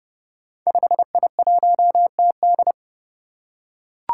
Morse code, heard as a keyed tone, about 28 wpm, 700 Hz. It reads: 5I1TD
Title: Five Challenging Letter-Number Combinations 30wpm